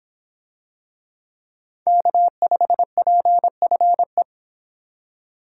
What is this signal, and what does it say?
Morse code, heard as a keyed tone, about 26 wpm, 700 Hz. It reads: K5PFE